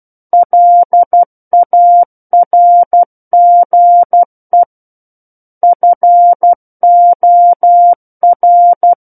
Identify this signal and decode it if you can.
Morse code, heard as a keyed tone, about 12 wpm, 700 Hz. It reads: LARGE FOR
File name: Book One / 03-Chapter 3 - 12 WPM